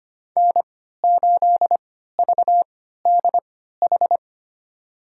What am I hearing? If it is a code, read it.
Morse code, heard as a keyed tone, about 25 wpm, spaced as for 14 wpm, 700 Hz. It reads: N8VDH
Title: USA Call Signs - 5 Characters - 3x Character Spacing - 25wpm